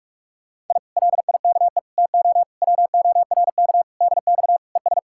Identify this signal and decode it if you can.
Morse code, heard as a keyed tone, about 48 wpm, 700 Hz. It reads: I LIKE TO WORK DX ES